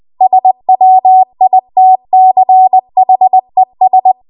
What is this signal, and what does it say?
Morse code, heard as a keyed tone, about 20 wpm, 750 Hz. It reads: SWITCHES
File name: Archive / 20 WPM / 20WPM